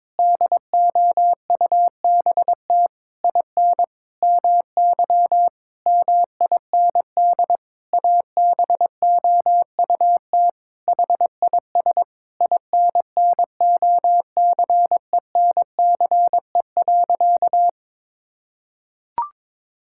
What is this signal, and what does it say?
Morse code, heard as a keyed tone, about 22 wpm, 700 Hz. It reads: DOUBT IN MY MIND ABOUT HIS INNOCENCE.
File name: Sentences with 7-Word Phrases - 22wpm